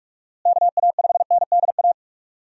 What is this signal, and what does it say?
Morse code, heard as a keyed tone, about 46 wpm, 700 Hz. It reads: KA5NDA
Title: USA Call Signs - 6 Characters - Repeated 3x - 45wpm